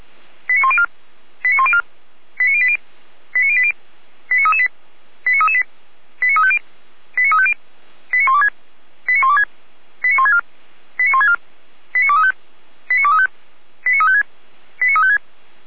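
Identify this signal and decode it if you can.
Five-tone selective call (ZVEI), heard as quick runs of five tones, tones of about 70 ms. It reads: 89104 89104 89080 89080 89338 89338 89350 89350 89126 89126 89164 89164 89235 89235 89356 89356
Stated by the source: ZVEI five-tone selective call received by SDR.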